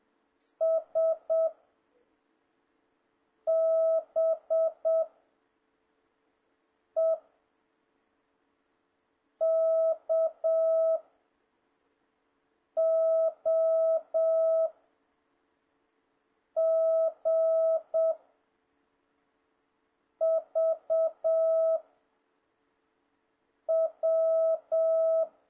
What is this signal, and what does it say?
Morse code, heard as a keyed tone, about 7 wpm, 650 Hz. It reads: S B E K O G V W
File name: jh8cbh / 2026cw